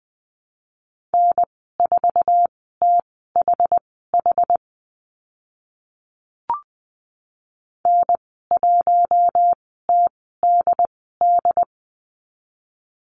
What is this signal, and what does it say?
Morse code, heard as a keyed tone, about 20 wpm, spaced as for 15 wpm, 700 Hz. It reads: N4THH N1TDD
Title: 076 Lesson 13g - Letter L - Call Signs 20wpm